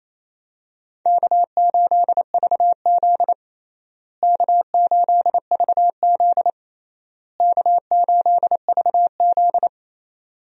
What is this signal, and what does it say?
Morse code, heard as a keyed tone, about 28 wpm, 700 Hz. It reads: K8VZ K8VZ K8VZ
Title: USA Call Signs - 4 Characters - Repeated 3x - 28wpm